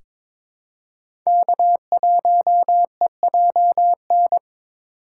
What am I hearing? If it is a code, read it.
Morse code, heard as a keyed tone, about 22 wpm, 700 Hz. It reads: K1EJN
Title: USA Call Signs - 5 Characters - 4x Character Spacing - 22wpm